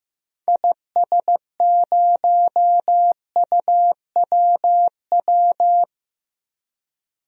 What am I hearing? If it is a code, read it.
Morse code, heard as a keyed tone, about 15 wpm, 700 Hz. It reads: IS0UWW
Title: DXCC-100 Call Signs - 6 Characters - 15wpm